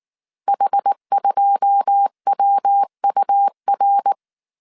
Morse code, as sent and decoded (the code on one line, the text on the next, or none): .... ..--- .-- ..- .-.
H2WUR